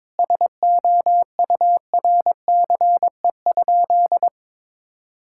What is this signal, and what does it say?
Morse code, heard as a keyed tone, about 22 wpm, 700 Hz. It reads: SOURCE?
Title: Sentences from Top 1000 Words - Review Part 5 - 22wpm